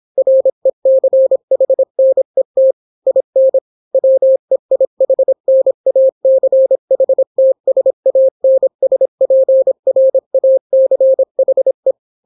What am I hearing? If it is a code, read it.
Morse code, heard as a keyed tone, about 26 wpm, 530 Hz. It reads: RECHNET IN WEIHNACHTSANSPRACHE